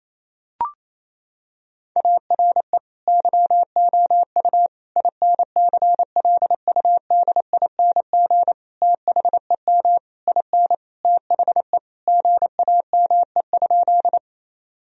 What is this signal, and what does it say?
Morse code, heard as a keyed tone, about 28 wpm, 700 Hz. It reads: ARE YOU INCLUDING THEM IN THE GAME?
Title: Sentences from Top 1000 Words - Review Part 4 - 28wpm